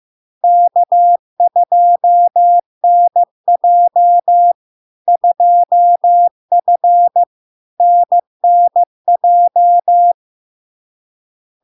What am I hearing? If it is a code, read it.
Morse code, heard as a keyed tone, about 15 wpm, 700 Hz. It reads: K2NJ 2F NNJ
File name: LARK-FD-2022-Log-in-CW-15wpm-700-Hz